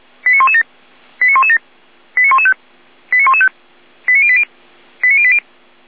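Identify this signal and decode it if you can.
Five-tone selective call (ZVEI), heard as quick runs of five tones, tones of about 70 ms. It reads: 89117 89117 89115 89115 89080 89080
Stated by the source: ZVEI five-tone selective call received on an SDR.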